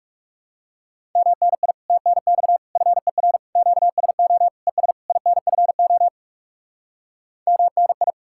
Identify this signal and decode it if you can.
Morse code, heard as a keyed tone, about 37 wpm, 700 Hz. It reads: MNI TNX FER QSO ES INFO MNI
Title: Speed-Racing Common QSO Phrases - 30wpm